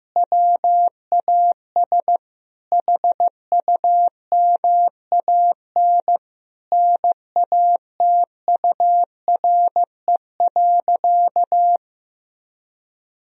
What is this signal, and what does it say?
Morse code, heard as a keyed tone, about 15 wpm, 700 Hz. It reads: WAS HUMAN NATURE.
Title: Sentences from Top 1000 Words - Review Part 3 - 15wpm